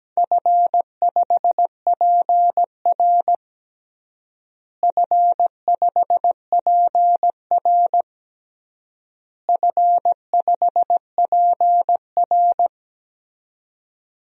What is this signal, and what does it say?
Morse code, heard as a keyed tone, about 17 wpm, 700 Hz. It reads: F5PR F5PR F5PR
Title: DXCC-100 Call Signs - 4 Characters - Repeated 3x - 17wpm